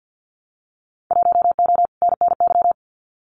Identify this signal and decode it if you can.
Morse code, heard as a keyed tone, about 54 wpm, 700 Hz. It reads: 1O NNY
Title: Winter Field Day Exchanges 50wpm